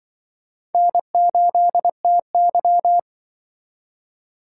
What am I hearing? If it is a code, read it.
Morse code, heard as a keyed tone, about 24 wpm, 700 Hz. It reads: N8TY
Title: Speed Racing - USA 1x2 Call Signs 20wpm